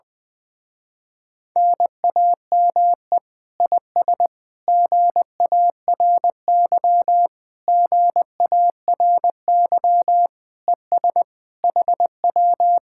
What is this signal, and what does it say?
Morse code, heard as a keyed tone, about 21 wpm, 700 Hz. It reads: NAME IS GARY GARY ES HW
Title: Speed-Racing Common QSO Phrases Encore - 20wpm